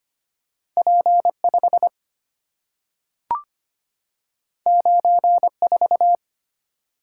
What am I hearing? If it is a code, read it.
Morse code, heard as a keyed tone, about 25 wpm, 700 Hz. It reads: P5 94
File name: Two Letter-Number Combinations - 25wpm